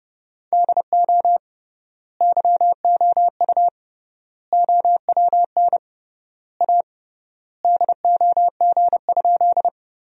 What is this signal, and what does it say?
Morse code, heard as a keyed tone, about 32 wpm, 700 Hz. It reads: DO YOU OWN A DOG?
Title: 3x Word-Spacing - Sentences from Top 400 Words - 30wpm